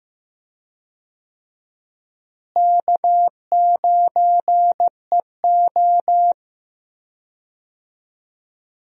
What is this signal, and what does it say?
Morse code, heard as a keyed tone, about 15 wpm, 700 Hz. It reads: K9EO
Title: USA Call Signs - 4 Characters - Repeated 3x - 15wpm